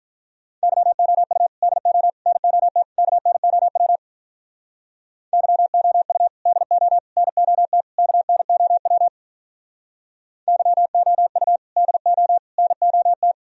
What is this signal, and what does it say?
Morse code, heard as a keyed tone, about 45 wpm, 700 Hz. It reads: YOU DO NOT KNOW YOU DO NOT KNOW YOU DO NOT